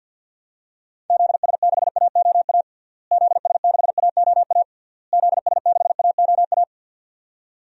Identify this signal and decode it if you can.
Morse code, heard as a keyed tone, about 51 wpm, 700 Hz. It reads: ZS6AOA ZS6AOA ZS6AOA
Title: DXCC-100 Call Signs - 6 Characters - Repeated 3x - 50wpm